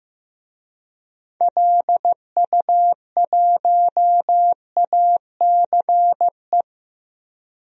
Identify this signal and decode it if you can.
Morse code, heard as a keyed tone, about 15 wpm, 700 Hz. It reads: LU1ACE
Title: DXCC-100 Call Signs - 6 Characters - 15wpm